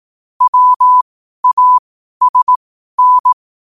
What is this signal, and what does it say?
Morse code, heard as a keyed tone, about 18 wpm, spaced as for 12 wpm, 1000 Hz. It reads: WASN
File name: 12wpm